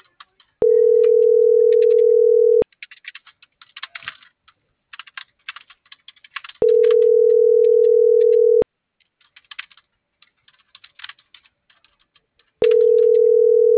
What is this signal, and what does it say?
Ringback tone (North America), 440 + 480 Hz.